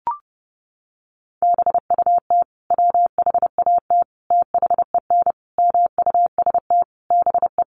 Morse code, heard as a keyed tone, about 30 wpm, 700 Hz. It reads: BUT WHAT THEN MUST BE
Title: Sentences from Top 500 Words 30wpm